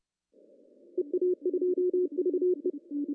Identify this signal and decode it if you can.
Morse code, heard as a keyed tone, about 30 wpm, 360 Hz. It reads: EA2VE
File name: CW / cqww-cw-k5tr